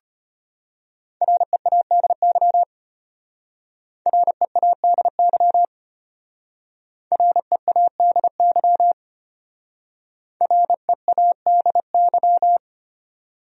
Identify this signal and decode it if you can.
Morse code, heard as a keyed tone, about 31 wpm, 700 Hz. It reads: READY READY READY READY
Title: Speed-Racing Top 500 Words 25wpm